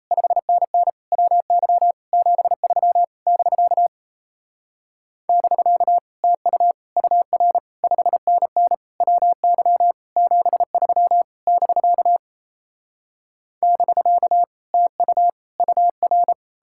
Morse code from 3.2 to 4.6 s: <BK>

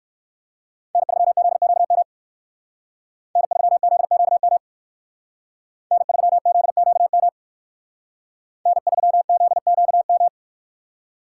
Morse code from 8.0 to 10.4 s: N3ZQM

